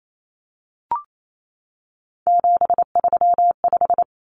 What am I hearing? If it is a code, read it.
Morse code, heard as a keyed tone, about 28 wpm, 700 Hz. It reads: 735
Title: Common Cut Numbers - 3-Digit 28wpm